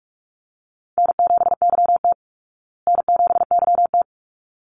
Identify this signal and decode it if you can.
Morse code, heard as a keyed tone, about 44 wpm, 700 Hz. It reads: N7YT N7YT
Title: Call Signs Repeated 3x - 45wpm